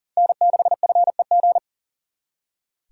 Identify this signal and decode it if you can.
Morse code, heard as a keyed tone, about 54 wpm, spaced as for 47 wpm, 700 Hz. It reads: N6FEG
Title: Speed Racing - USA 1x3 Call Signs 40wpm